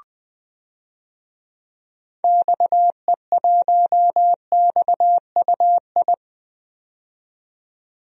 Morse code, heard as a keyed tone, about 20 wpm, 700 Hz. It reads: XE1XUI
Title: DXCC-100 Call Signs - 6 Characters - 20wpm